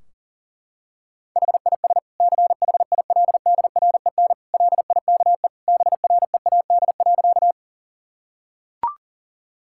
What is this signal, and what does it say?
Morse code, heard as a keyed tone, about 37 wpm, 700 Hz. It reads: HIS CHILDREN LIKE BREAD.
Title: Sentences from Top 1600 Words 40wpm